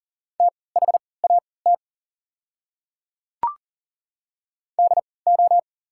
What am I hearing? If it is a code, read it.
Morse code, heard as a keyed tone, about 40 wpm, spaced as for 23 wpm, 700 Hz. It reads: THAT DO
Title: 3x Character Spacing - Top 100 Words - 40wpm